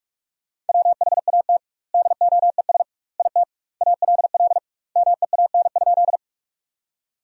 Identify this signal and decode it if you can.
Morse code, heard as a keyed tone, about 41 wpm, 700 Hz. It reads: WHAT DOES IT ALL MEAN?
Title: 2x Word-Spacing - Sentences from Top 200 Words - 45wpm